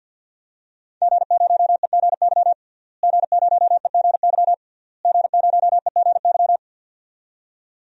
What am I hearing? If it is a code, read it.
Morse code, heard as a keyed tone, about 51 wpm, 700 Hz. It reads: G0EGY G0EGY G0EGY
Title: DXCC-100 Call Signs - 5 Characters - Repeated 3x - 50wpm